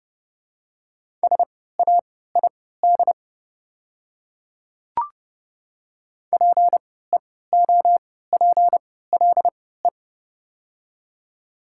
Morse code, heard as a keyed tone, about 30 wpm, spaced as for 17 wpm, 700 Hz. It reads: SAID PEOPLE
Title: 3x Character Spacing - Top 100 Words - 30wpm